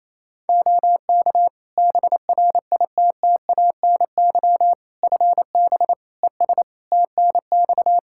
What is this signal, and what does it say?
Morse code, heard as a keyed tone, about 28 wpm, 700 Hz. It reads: OK BRITTANY FB ES TNX
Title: Practice QSO - Exchange 2 - Info - 28wpm